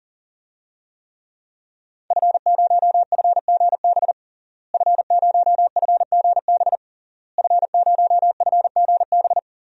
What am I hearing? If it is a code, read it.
Morse code, heard as a keyed tone, about 40 wpm, 700 Hz. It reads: F0FGB F0FGB F0FGB
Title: DXCC-100 Call Signs - 5 Characters - Repeated 3x - 40wpm